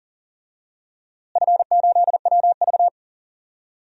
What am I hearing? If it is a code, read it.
Morse code, heard as a keyed tone, about 40 wpm, 700 Hz. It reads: F8WV
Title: DXCC-100 Call Signs - 4 Characters - Repeated 2x - 40wpm